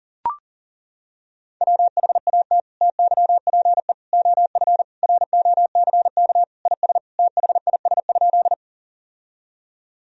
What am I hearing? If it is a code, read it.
Morse code, heard as a keyed tone, about 37 wpm, 700 Hz. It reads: WHAT TYPE OF ROCK IS THIS?
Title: Sentences from Top 1000 Words - Review Part 3 - 40wpm